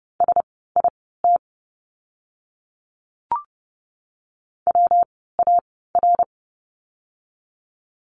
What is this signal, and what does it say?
Morse code, heard as a keyed tone, about 30 wpm, spaced as for 17 wpm, 700 Hz. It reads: SIT WAR
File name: course / 30:10wpm / 115 Lesson 19f - Letter w - Words 30wpm